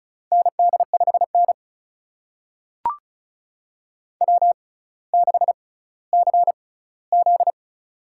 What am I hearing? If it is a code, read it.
Morse code, heard as a keyed tone, about 36 wpm, spaced as for 18 wpm, 700 Hz. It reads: ND5N W6CZ
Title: USA Call Signs - 4 Characters - 6x Character Spacing - 35wpm